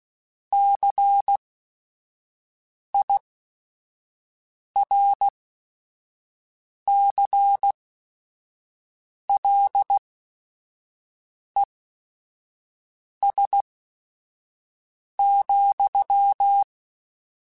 Morse code, heard as a keyed tone, about 16 wpm, spaced as for 5 wpm, 800 Hz. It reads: CIRCLES,
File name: Technical Subjects / Technical Subjects-mp3s / XC @5